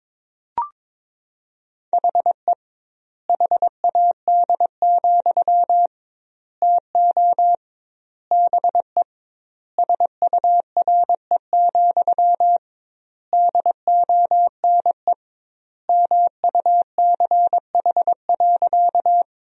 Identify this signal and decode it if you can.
Morse code, heard as a keyed tone, about 22 wpm, 700 Hz. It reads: HE HAD, TO BE SURE, DONE MUCH.
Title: 2x Word-Spacing - Sentences from Top 500 Words - 22wpm